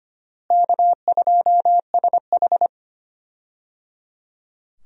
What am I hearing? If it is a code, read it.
Morse code, heard as a keyed tone, about 25 wpm, 700 Hz. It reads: K2SH
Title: USA Call Signs - 4 Characters - Repeated 3x - 25wpm